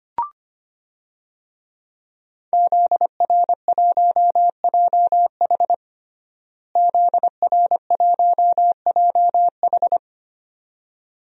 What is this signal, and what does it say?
Morse code, heard as a keyed tone, about 25 wpm, 700 Hz. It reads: ZR1JH ZR1JH
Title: DXCC-100 Call Signs - 5 Characters - Repeated 2x - 25wpm